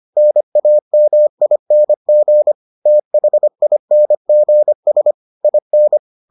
Morse code, heard as a keyed tone, about 25 wpm, 600 Hz. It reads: NAMING THINGS IN